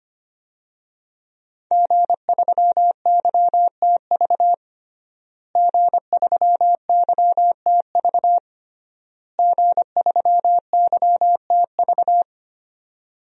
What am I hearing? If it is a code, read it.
Morse code, heard as a keyed tone, about 25 wpm, 700 Hz. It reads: G3YTV G3YTV G3YTV
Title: DXCC-100 Call Signs - 5 Characters - Repeated 3x - 25wpm